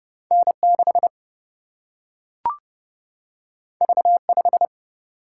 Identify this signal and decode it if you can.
Morse code, heard as a keyed tone, about 30 wpm, 700 Hz. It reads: N6 V5
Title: Single-Letter Single-Number 30wpm